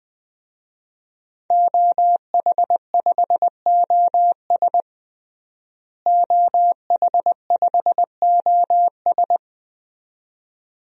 Morse code, heard as a keyed tone, about 20 wpm, 700 Hz. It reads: OH5OS OH5OS
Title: DXCC-100 Call Signs - 5 Characters - Repeated 2x - 20wpm